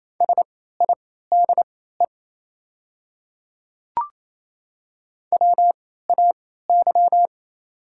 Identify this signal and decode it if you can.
Morse code, heard as a keyed tone, about 27 wpm, spaced as for 16 wpm, 700 Hz. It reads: SIDE WAY
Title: 3x Character Spacing - Top 200 Words - 28wpm